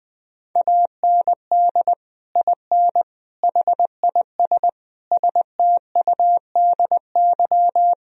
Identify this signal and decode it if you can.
Morse code, heard as a keyed tone, about 20 wpm, 700 Hz. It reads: AND IN HIS STUDY
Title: Sentences from Top 1000 Words - Review Part 1 - 20wpm